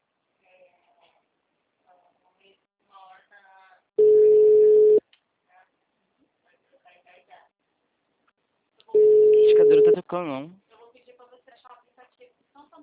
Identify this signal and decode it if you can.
Ringback tone (Europe, 425 Hz), with speech.